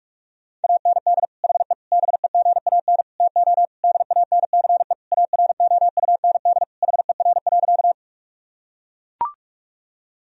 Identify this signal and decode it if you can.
Morse code, heard as a keyed tone, about 43 wpm, 700 Hz. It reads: AND HE BEGAN TO DANCE AROUND HER.
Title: Sentences from Top 1000 Words - Review Part 4 - 45wpm